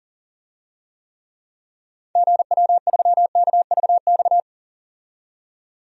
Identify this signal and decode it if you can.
Morse code, heard as a keyed tone, about 40 wpm, 700 Hz. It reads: GW3KVX
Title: DXCC-100 Call Signs - 6 Characters - 40wpm